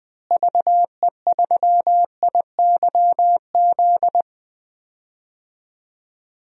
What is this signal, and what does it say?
Morse code, heard as a keyed tone, about 20 wpm, 700 Hz. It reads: VE3IYZ
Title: DXCC-100 Call Signs - 6 Characters - Repeated 3x - 20wpm